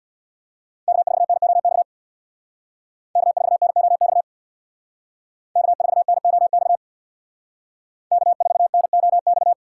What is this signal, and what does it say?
Morse code, heard as a keyed tone, about 64 wpm, 700 Hz. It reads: K4NOX K4NOX K4NOX K4NOX